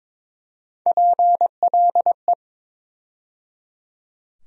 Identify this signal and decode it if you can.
Morse code, heard as a keyed tone, about 21 wpm, 700 Hz. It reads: PLE
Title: USA Call Sign Postfixes - 3 Letters 22wpm